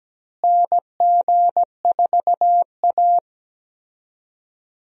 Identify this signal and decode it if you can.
Morse code, heard as a keyed tone, about 17 wpm, 700 Hz. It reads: NG4A
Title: USA Call Signs - 4 Characters - Repeated 3x - 17wpm